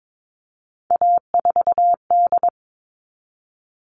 Morse code, heard as a keyed tone, about 22 wpm, 700 Hz. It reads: A4D
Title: All Cut Numbers - 3-Digit 22wpm